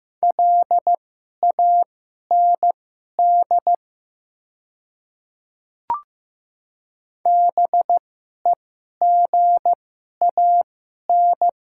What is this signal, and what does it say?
Morse code, heard as a keyed tone, about 15 wpm, spaced as for 11 wpm, 700 Hz. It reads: LAND BEGAN
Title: 2x Character Spacing - Top 300 Words - 15wpm